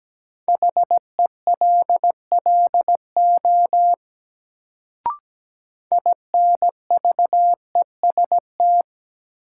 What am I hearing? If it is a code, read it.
Morse code, heard as a keyed tone, about 17 wpm, 700 Hz. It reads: HELLO INVEST
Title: Sentences from Top 1100 Words - New Word Drill Alt - 17wpm